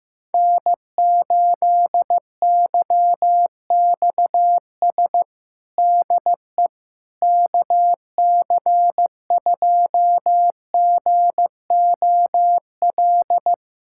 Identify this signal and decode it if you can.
Morse code, heard as a keyed tone, about 15 wpm, 700 Hz. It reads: N8YXS DE KC2GOL
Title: USA Callsign DE Callsign 15wpm